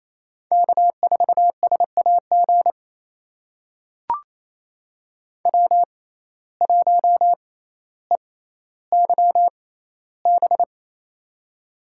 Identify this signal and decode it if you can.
Morse code, heard as a keyed tone, about 28 wpm, spaced as for 14 wpm, 700 Hz. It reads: K4SAG W1EYB
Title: USA Call Signs - 6x Character Spacing - 28wpm